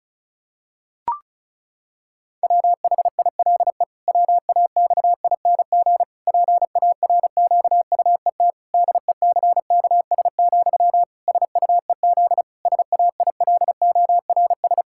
Morse code, heard as a keyed tone, about 34 wpm, 700 Hz. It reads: WHILE WAXING PARQUET DECKS, SUEZ SAILORS
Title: Pangram Challenge v2 - 35wpm